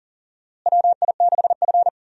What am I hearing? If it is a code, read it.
Morse code, heard as a keyed tone, about 38 wpm, 700 Hz. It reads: WI6F
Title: USA Call Signs - 4 Characters - Repeated 2x - 40wpm